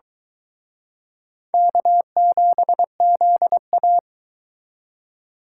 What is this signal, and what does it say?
Morse code, heard as a keyed tone, about 23 wpm, 700 Hz. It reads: K7ZA